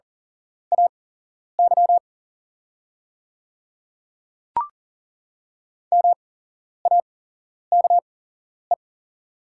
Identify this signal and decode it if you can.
Morse code, heard as a keyed tone, about 40 wpm, spaced as for 11 wpm, 700 Hz. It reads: AY MAKE
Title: 8x Character Spacing - Top 100 Words - 40wpm